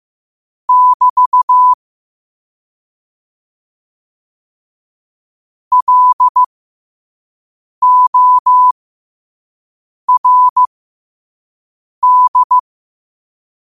Morse code, heard as a keyed tone, about 15 wpm, spaced as for 5 wpm, 1000 Hz. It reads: <BT> LORD
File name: qotdfiles / 5wpm